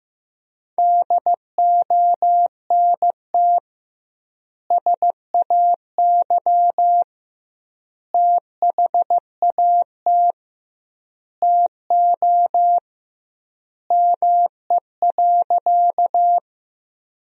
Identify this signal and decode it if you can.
Morse code, heard as a keyed tone, about 15 wpm, 700 Hz. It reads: DONT SAY THAT TO ME.
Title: 2x Word-Spacing - Sentences from Top 300 Words - 15wpm